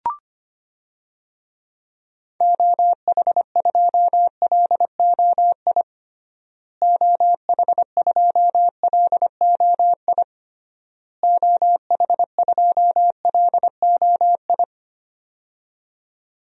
Morse code, heard as a keyed tone, about 25 wpm, 700 Hz. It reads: OH2LOI OH2LOI OH2LOI
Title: DXCC-100 Call Signs - 6 Characters - Repeated 3x - 25wpm